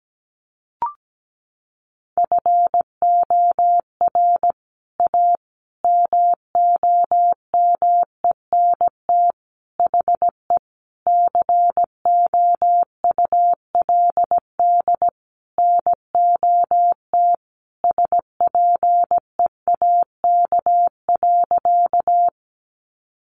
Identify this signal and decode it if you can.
Morse code, heard as a keyed tone, about 17 wpm, 700 Hz. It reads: FOR A MOMENT HE COULD NOT SPEAK.
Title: Sentences from Top 1100 Words 17wpm